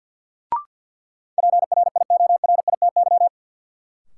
Morse code, heard as a keyed tone, about 47 wpm, 700 Hz. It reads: PRIORITY
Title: Sentences from Top 1600 Words - New Word Drill Alt - 50wpm